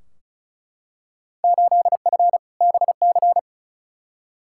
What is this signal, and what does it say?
Morse code, heard as a keyed tone, about 36 wpm, 700 Hz. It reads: 8F BC
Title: Field Day Exchanges Encore - 35wpm